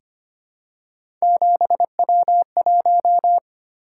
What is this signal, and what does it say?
Morse code, heard as a keyed tone, about 25 wpm, 700 Hz. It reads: 7W1
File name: Three Letter-Number Combinations - 25wpm